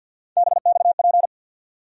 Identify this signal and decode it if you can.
Morse code, heard as a keyed tone, about 46 wpm, 690 Hz. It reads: BXP
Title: USA Call Sign Postfixes - 3 Letters 50wpm